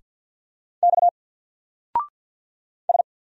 Morse code, heard as a keyed tone, about 49 wpm, 700 Hz. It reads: X S
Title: Single Character (Letter:Number:Punctuation:Prosign) 50wpm